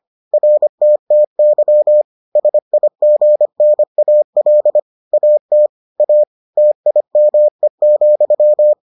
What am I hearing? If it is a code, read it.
Morse code, heard as a keyed tone, about 25 wpm, 600 Hz. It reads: RTTY SIGNAL AT A TIME,